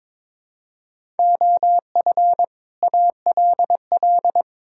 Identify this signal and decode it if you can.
Morse code, heard as a keyed tone, about 23 wpm, 700 Hz. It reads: OF ALL
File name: Build A Phrase - 2-Word Phrases - 22wpm